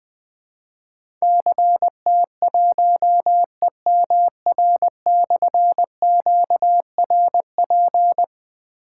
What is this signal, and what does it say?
Morse code, heard as a keyed tone, about 20 wpm, 700 Hz. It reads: CT1EMR/QRP
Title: Difficult Call Signs 20wpm